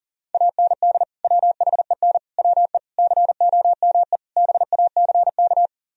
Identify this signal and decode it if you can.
Morse code, heard as a keyed tone, about 48 wpm, 700 Hz. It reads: AND WHEN WE COME BACK
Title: Speed-Racing - Sets of 5 Words 40wpm